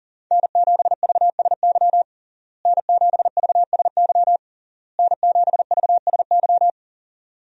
Morse code, heard as a keyed tone, about 40 wpm, 700 Hz. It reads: N7VSY N7VSY N7VSY
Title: USA Call Signs - 5 Characters - Repeated 3x - 40wpm